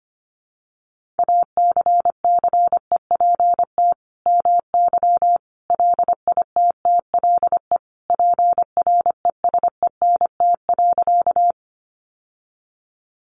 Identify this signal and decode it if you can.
Morse code, heard as a keyed tone, about 25 wpm, 700 Hz. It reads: ACCEPT MY LITTLE PRESENT.
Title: Sentences from Top 1300 Words 25wpm